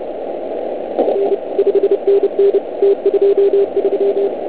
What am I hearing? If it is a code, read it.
Morse code, heard as a keyed tone, about 31 wpm, 370 Hz, with noise. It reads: R 5NN T23